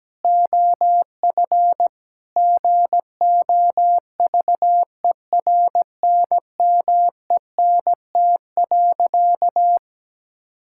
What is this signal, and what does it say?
Morse code, heard as a keyed tone, about 17 wpm, 700 Hz. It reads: OF GOVERNMENT.
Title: Sentences from Top 1000 Words - Review Part 5 - 17wpm